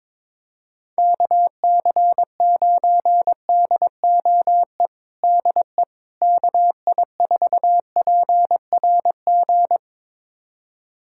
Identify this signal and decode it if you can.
Morse code, heard as a keyed tone, about 22 wpm, 700 Hz. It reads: KC9DOE DE KI4PRG
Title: USA Callsign DE Callsign - 6 Characters - 22wpm